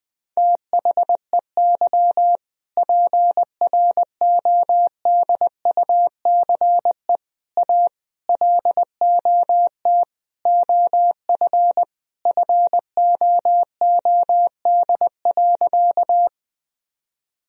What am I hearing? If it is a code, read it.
Morse code, heard as a keyed tone, about 21 wpm, 700 Hz. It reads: THEY PRODUCE A LOT OF FOOD.
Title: Sentences from Top 1000 Words - Review Part 3 - 20wpm